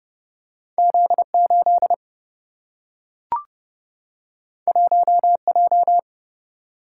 Morse code, heard as a keyed tone, about 30 wpm, 700 Hz. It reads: Z8 1J
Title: Single-Letter Single-Number 30wpm